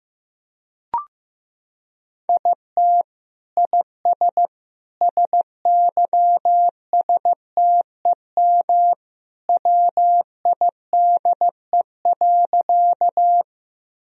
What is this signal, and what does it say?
Morse code, heard as a keyed tone, about 15 wpm, 700 Hz. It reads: IT IS SYSTEM WIDE.
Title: Sentences from Top 1000 Words - Review Part 3 - 15wpm